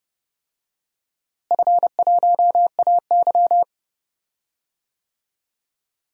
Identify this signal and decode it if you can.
Morse code, heard as a keyed tone, about 30 wpm, 700 Hz. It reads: F1AY
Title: DXCC-100 Call Signs - 4 Characters - Repeated 3x - 30wpm